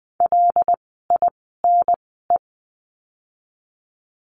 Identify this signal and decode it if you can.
Morse code, heard as a keyed tone, about 20 wpm, spaced as for 15 wpm, 700 Hz. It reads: LINE
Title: 115 Lesson 19f - Letter w - Words 20wpm